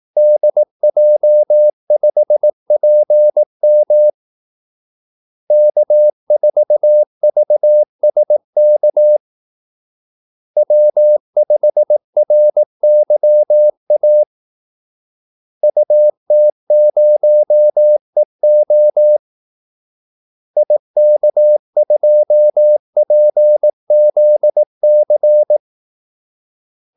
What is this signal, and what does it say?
Morse code, heard as a keyed tone, about 18 wpm, 600 Hz. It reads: DJ5PM K4VSK W5RYA UT0EO IK2PZC
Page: Kaldesignaler 18 wpm